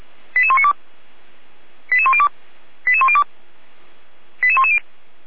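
Five-tone selective call (ZVEI), heard as quick runs of five tones, tones of about 70 ms. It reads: 88192 88102 88102 88110
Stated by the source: ZVEI five-tone call received on SDR, code 88109.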